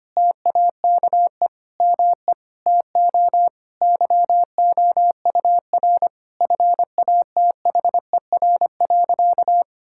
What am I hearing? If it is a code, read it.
Morse code, heard as a keyed tone, about 25 wpm, 700 Hz. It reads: TAKE ME TO YOUR FATHER.
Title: Sentences from Top 1000 Words - Review Part 6 - 25wpm